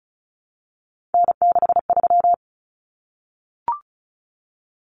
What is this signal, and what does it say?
Morse code, heard as a keyed tone, about 34 wpm, 700 Hz. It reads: N63
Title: Common Cut Numbers - 3-Digit 35wpm